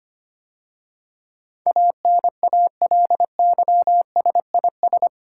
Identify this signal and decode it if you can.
Morse code, heard as a keyed tone, about 25 wpm, 700 Hz. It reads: ANALYSIS
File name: 8 Letter Words - First 3 Letters Given - 25wpm